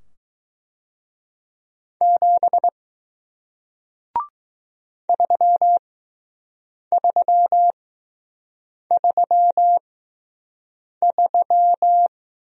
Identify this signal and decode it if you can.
Morse code, heard as a keyed tone, about 20 wpm, spaced as for 6 wpm, 700 Hz. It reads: 7 3333